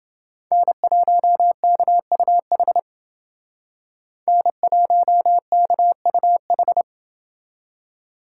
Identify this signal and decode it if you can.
Morse code, heard as a keyed tone, about 28 wpm, 700 Hz. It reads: N1KUH N1KUH